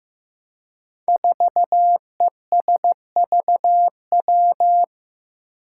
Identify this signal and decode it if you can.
Morse code, heard as a keyed tone, about 15 wpm, 700 Hz. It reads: 4ESVW
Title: Five Challenging Letter-Number Combinations 15wpm